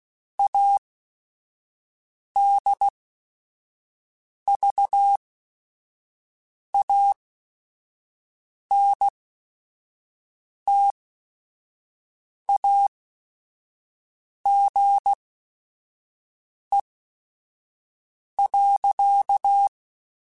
Morse code, heard as a keyed tone, about 16 wpm, spaced as for 5 wpm, 800 Hz. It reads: ADVANTAGE.
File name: Technical Subjects / Technical Subjects-mp3s / Sw Pwr Sup @5